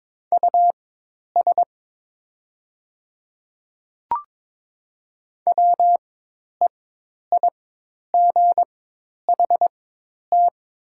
Morse code, heard as a keyed tone, about 22 wpm, spaced as for 10 wpm, 700 Hz. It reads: US WEIGHT
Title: 4x Character Spacing - Top 500 Words - 22wpm